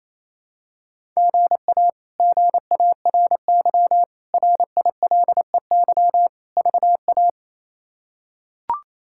Morse code, heard as a keyed tone, about 28 wpm, 700 Hz. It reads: GA GARY RILEY VA
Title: K1USN SST QSO Practice - 28wpm